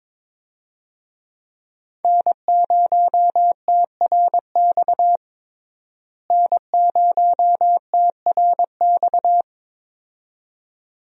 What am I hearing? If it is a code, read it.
Morse code, heard as a keyed tone, about 21 wpm, 700 Hz. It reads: N0TRX N0TRX
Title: USA Call Signs - 5 Characters - Repeated 2x - 22wpm